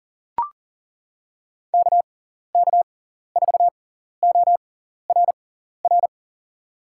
Morse code, heard as a keyed tone, about 40 wpm, spaced as for 14 wpm, 700 Hz. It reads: KK4ORR
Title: USA Call Signs - 6 Characters - 6x Character Spacing - 40wpm